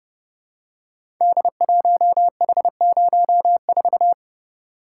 Morse code, heard as a keyed tone, about 30 wpm, 700 Hz. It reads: D1H04
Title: Code Groups - 5 Letter:Numbers - 30wpm